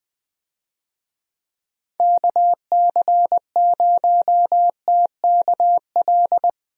Morse code, heard as a keyed tone, about 20 wpm, 700 Hz. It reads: KC0TKL
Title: USA Call Signs - 6 Characters - 20wpm